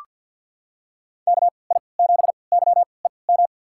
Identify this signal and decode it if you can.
Morse code, heard as a keyed tone, about 46 wpm, spaced as for 27 wpm, 700 Hz. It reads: KI7YEM
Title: USA Call Signs - 6 Characters - 3x Character Spacing - 50wpm